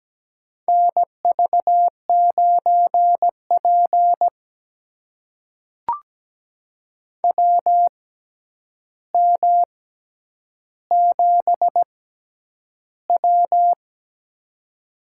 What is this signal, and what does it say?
Morse code, heard as a keyed tone, about 17 wpm, spaced as for 9 wpm, 700 Hz. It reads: NV9P WM7W
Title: USA Call Signs - 4 Characters - 6x Character Spacing - 17wpm